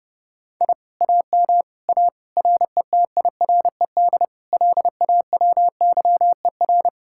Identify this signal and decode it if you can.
Morse code, heard as a keyed tone, about 40 wpm, 700 Hz. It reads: I AM A RETIRED LAWYER